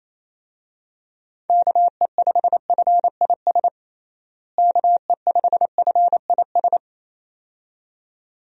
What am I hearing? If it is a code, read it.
Morse code, heard as a keyed tone, about 28 wpm, 700 Hz. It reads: KE5FIS KE5FIS
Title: USA Call Signs - 6 Characters - Repeated 2x - 28wpm